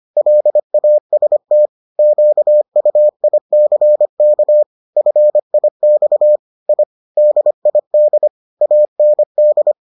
Morse code, heard as a keyed tone, about 25 wpm, 600 Hz. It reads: LAST QUICK FIX I DID AND